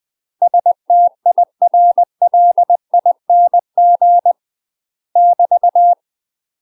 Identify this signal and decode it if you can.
Morse code, heard as a keyed tone, about 20 wpm, 700 Hz. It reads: STIRLING <BT>
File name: UK MASTER QSOs TRAINING / 20WPM / UK MASTER QSOs 0005